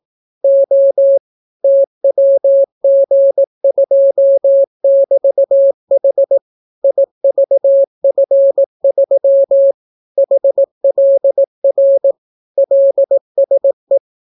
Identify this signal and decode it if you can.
Morse code, heard as a keyed tone, about 18 wpm, 550 Hz. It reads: O TWG2<BT>H IVF3 HLR LSE